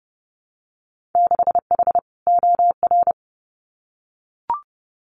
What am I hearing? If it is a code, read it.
Morse code, heard as a keyed tone, about 30 wpm, 700 Hz. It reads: 6H OR
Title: Winter Field Day Exchanges Encore 30wpm